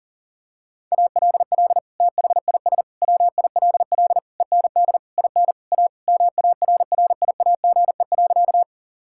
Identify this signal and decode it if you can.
Morse code, heard as a keyed tone, about 40 wpm, 700 Hz. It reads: ALL THIS WILL END IN A MARRIAGE.
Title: Sentences from Top 1000 Words - Review Part 5 - 40wpm